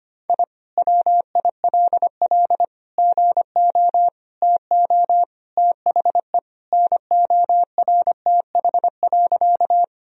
Morse code, heard as a keyed tone, about 25 wpm, 700 Hz. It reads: I WILL GO TO THE NORTH.
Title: Sentences from Top 1000 Words - Review Part 1 - 25wpm